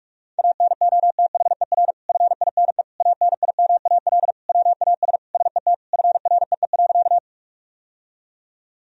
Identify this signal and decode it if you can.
Morse code, heard as a keyed tone, about 41 wpm, 700 Hz. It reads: ANOTHER FINE ANIMAL WAS SET FREE.
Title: Sentences from Top 1000 Words - Review Part 1 - 45wpm